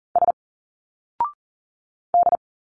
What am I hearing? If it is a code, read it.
Morse code, heard as a keyed tone, about 40 wpm, 700 Hz. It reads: S D
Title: Single Letters 40wpm